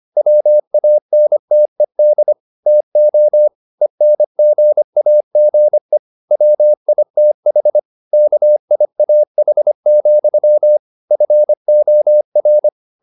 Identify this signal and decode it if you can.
Morse code, heard as a keyed tone, about 25 wpm, 600 Hz. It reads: WANTED TO ENGAGE WITH KIAH, FOR